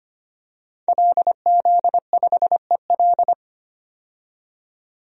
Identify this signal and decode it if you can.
Morse code, heard as a keyed tone, about 25 wpm, 700 Hz. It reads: LZ5EL
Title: DXCC-100 Call Signs - 5 Characters - 25wpm